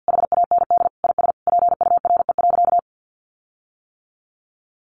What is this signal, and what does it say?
Morse code, heard as a keyed tone, about 50 wpm, 700 Hz. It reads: HAND IS PURE.
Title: Sentences from Top 1500 Words 50wpm